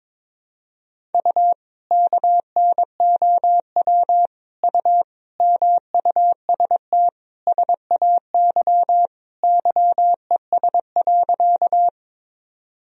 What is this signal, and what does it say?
Morse code, heard as a keyed tone, about 22 wpm, 700 Hz. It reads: U KNOW U MUST SAY YES.
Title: Sentences from Top 500 Words and Abbreviations 22wpm